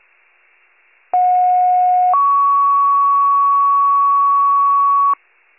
Two-tone page, heard as a one-second tone, then a three-second tone, about 730 then 1120 Hz. It is heard from 1.1 to 5.1 s.